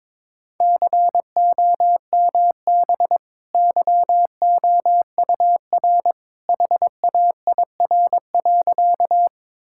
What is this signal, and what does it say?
Morse code, heard as a keyed tone, about 22 wpm, 700 Hz. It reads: COMB YOUR HAIR.
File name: practice / 22wpm / Easy Sentences with Hints - 22wpm